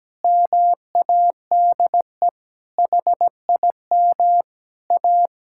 Morse code, heard as a keyed tone, about 17 wpm, 700 Hz. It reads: MADE HIM A
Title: Sentences from Top 1000 Words - Review Part 5 - 17wpm